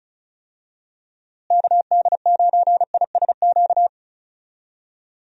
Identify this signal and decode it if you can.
Morse code, heard as a keyed tone, about 36 wpm, 700 Hz. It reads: KD9ISQ
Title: USA Call Signs - 6 Characters - Repeated 3x - 35wpm